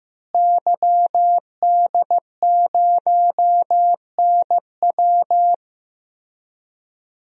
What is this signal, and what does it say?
Morse code, heard as a keyed tone, about 15 wpm, 700 Hz. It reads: YD0NW
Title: DXCC-100 Call Signs - 5 Characters - Repeated 3x - 15wpm